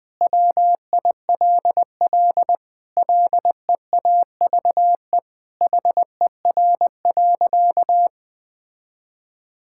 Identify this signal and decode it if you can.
Morse code, heard as a keyed tone, about 20 wpm, 700 Hz. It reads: WILL LEAVE HER.
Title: Sentences from Top 1000 Words - Review Part 5 - 20wpm